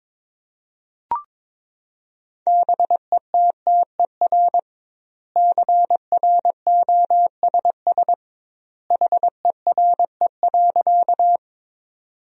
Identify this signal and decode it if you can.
Morse code, heard as a keyed tone, about 22 wpm, 700 Hz. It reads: BETTER CROSS HERE.
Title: 2x Word-Spacing - Sentences from Top 400 Words - 22wpm